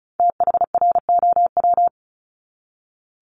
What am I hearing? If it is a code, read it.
Morse code, heard as a keyed tone, about 34 wpm, 700 Hz. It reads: THROW
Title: Top 1501-1750 Words 35wpm